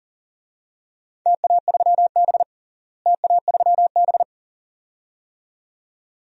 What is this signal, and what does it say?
Morse code, heard as a keyed tone, about 40 wpm, 700 Hz. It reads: TA3B TA3B
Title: DXCC-100 Call Signs - 4 Characters - Repeated 2x - 40wpm